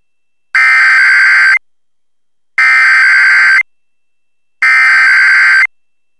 Emergency Alert System (SAME) tones (header bursts), heard as harsh buzzing bursts. It is heard from 0.6 to 5.1 s.